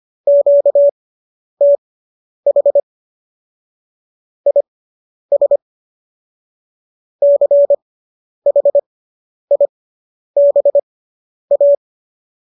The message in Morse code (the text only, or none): QTH IS CHIBA